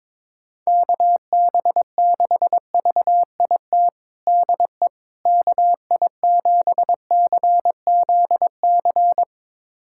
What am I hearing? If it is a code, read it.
Morse code, heard as a keyed tone, about 22 wpm, 700 Hz. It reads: KB6VIT DE KI7CZC
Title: USA Callsign DE Callsign - 6 Characters - 22wpm